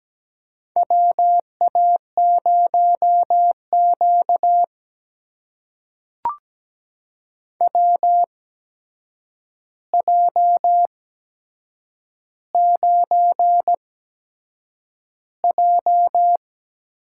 Morse code, heard as a keyed tone, about 17 wpm, spaced as for 7 wpm, 700 Hz. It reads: WA0Q WJ9J
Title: USA Call Signs - 4 Characters - 8x Character Spacing - 17wpm